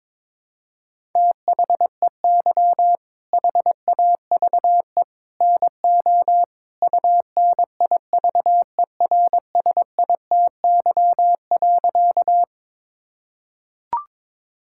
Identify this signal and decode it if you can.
Morse code, heard as a keyed tone, about 21 wpm, 700 Hz. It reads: THEY HAVE NO UNIVERSITY.
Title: Sentences from Top 1000 Words - Review Part 5 - 22wpm